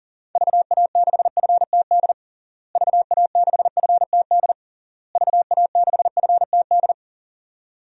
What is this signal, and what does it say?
Morse code, heard as a keyed tone, about 40 wpm, 700 Hz. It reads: VA6FTD VA6FTD VA6FTD
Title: DXCC-100 Call Signs - 6 Characters - Repeated 3x - 40wpm